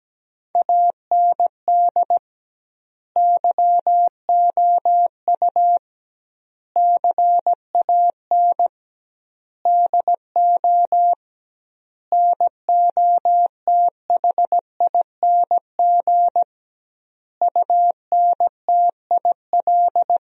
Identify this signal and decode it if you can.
Morse code, heard as a keyed tone, about 17 wpm, 700 Hz. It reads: AND YOU CAN DO NOTHING UNTIL
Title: 2x Word-Spacing - Sentences from Top 500 Words - 17wpm